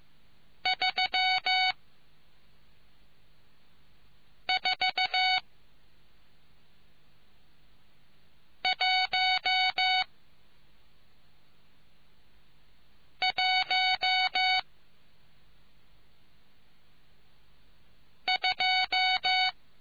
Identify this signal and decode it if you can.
Morse code, heard as a keyed tone, about 15 wpm, spaced as for 3 wpm, 750 Hz. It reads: 34112